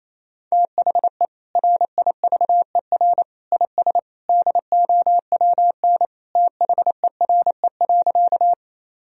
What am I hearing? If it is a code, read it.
Morse code, heard as a keyed tone, about 28 wpm, 700 Hz. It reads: THE RIVER IS DOWN THERE.
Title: Sentences from Top 1000 Words - Review Part 1 - 28wpm